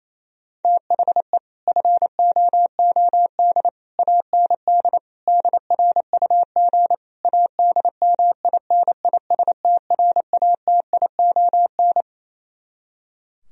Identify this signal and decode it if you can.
Morse code, heard as a keyed tone, about 26 wpm, 700 Hz. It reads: THE FOOD AND DRUG ADMINISTRATION
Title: Speed-Racing - Sets of 5 Words 28wpm